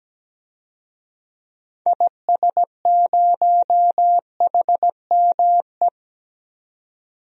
Morse code, heard as a keyed tone, about 17 wpm, 700 Hz. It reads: IS0HME
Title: DXCC-100 Call Signs - 6 Characters - 17wpm